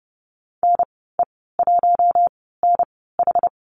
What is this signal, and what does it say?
Morse code, heard as a keyed tone, about 30 wpm, spaced as for 17 wpm, 700 Hz. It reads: NE1NH
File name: Lesson 14g - Number 2 - Call Signs 30wpm